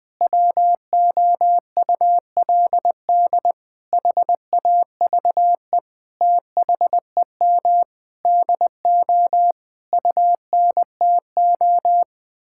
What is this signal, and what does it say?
Morse code, heard as a keyed tone, about 20 wpm, 700 Hz. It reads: WOULD HAVE THEM DO UNTO
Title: Sentences with 8-Word Phrases - 20wpm